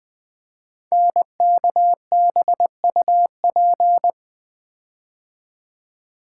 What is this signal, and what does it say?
Morse code, heard as a keyed tone, about 20 wpm, 700 Hz. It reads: NKBUP
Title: Code Groups - 5 Letters - 20wpm